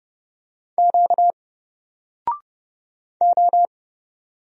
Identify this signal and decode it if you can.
Morse code, heard as a keyed tone, about 29 wpm, 700 Hz. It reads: Q O
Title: Single Character (Letter:Number:Punctuation:Prosign) 30wpm